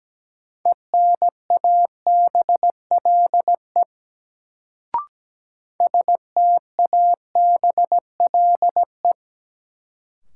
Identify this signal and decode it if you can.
Morse code, heard as a keyed tone, about 17 wpm, 700 Hz. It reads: ENABLE STABLE
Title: Sentences from Top 1500 Words - New Word Drill - 17wpm